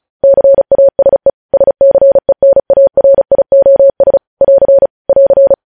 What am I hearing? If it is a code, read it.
Morse code, heard as a keyed tone, about 36 wpm, 550 Hz. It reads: CASE SCENARIOS <AR> <AR>
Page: news headlines in morse code at 35 wpm